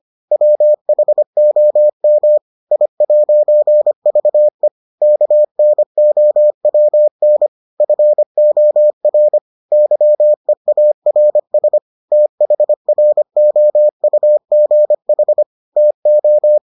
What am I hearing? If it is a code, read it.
Morse code, heard as a keyed tone, about 25 wpm, 600 Hz. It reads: WHOM I'VE KNOWN FOR YEARS THROUGH TO